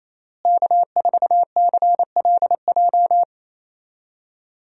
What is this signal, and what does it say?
Morse code, heard as a keyed tone, about 28 wpm, 700 Hz. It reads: K4CLJ
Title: USA Call Signs - 5 Characters - Repeated 3x - 28wpm